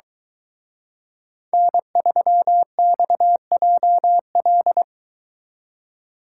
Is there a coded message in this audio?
Yes: Morse code from 1.5 to 4.8 s, heard as a keyed tone, about 23 wpm, 700 Hz. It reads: N3XJL